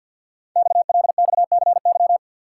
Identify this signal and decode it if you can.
Morse code, heard as a keyed tone, about 50 wpm, 700 Hz. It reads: XLXCY
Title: Code Groups - 5 Letters - 50wpm